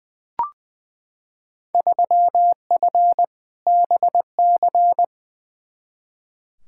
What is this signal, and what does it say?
Morse code, heard as a keyed tone, about 20 wpm, 700 Hz. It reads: 3F BC
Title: Field Day Exchanges Encore - 20wpm